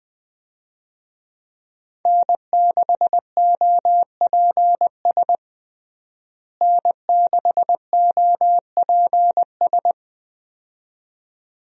Morse code, heard as a keyed tone, about 20 wpm, 700 Hz. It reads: N6OPS N6OPS
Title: USA Call Signs - 5 Characters - Repeated 2x - 20wpm